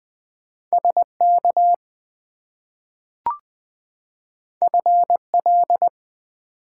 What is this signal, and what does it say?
Morse code, heard as a keyed tone, about 20 wpm, 700 Hz. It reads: SK FL
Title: US States and Canadian Provinces 20wpm